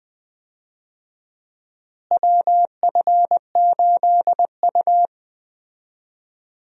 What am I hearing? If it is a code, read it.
Morse code, heard as a keyed tone, about 20 wpm, 700 Hz. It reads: WF8U
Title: USA Call Signs - 4 Characters - Repeated 2x - 20wpm